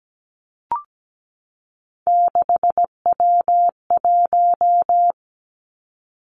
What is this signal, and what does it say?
Morse code, heard as a keyed tone, about 17 wpm, 700 Hz. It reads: 6W1
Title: Three Challenging Letter-Number Combinations 17wpm